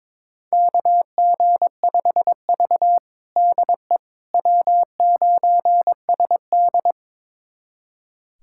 Morse code, heard as a keyed tone, about 22 wpm, 700 Hz. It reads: KG5V DE W9SD
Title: USA Callsign DE Callsign - 4 Characters - 22wpm